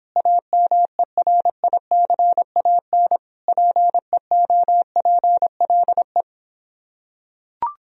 Morse code, heard as a keyed tone, about 26 wpm, 700 Hz. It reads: AMERICAN PEOPLE